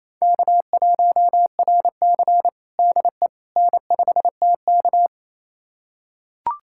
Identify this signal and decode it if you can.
Morse code, heard as a keyed tone, about 28 wpm, 700 Hz. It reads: K1RC DE N5TK
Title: USA Callsign DE Callsign - 4 Characters - 28wpm